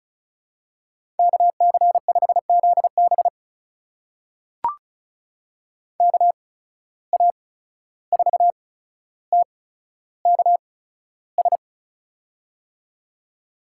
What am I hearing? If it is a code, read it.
Morse code, heard as a keyed tone, about 36 wpm, spaced as for 14 wpm, 700 Hz. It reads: KC5ZB KA4TKS
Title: USA Call Signs - 8x Character Spacing - 35wpm